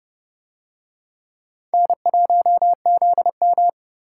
Morse code, heard as a keyed tone, about 27 wpm, 700 Hz. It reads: N1ZM